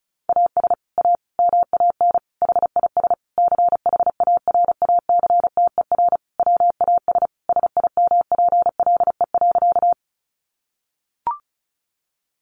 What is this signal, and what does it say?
Morse code, heard as a keyed tone, about 34 wpm, 700 Hz. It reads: AS A MAN HIS CHARACTER WAS SIMPLE.
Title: Sentences from Top 1200 Words 35wpm